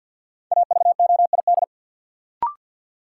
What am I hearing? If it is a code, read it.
Morse code, heard as a keyed tone, about 52 wpm, 700 Hz. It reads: AVOID
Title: Sentences from Top 1000 Words - Review Part 6 - New Word Drill - 50wpm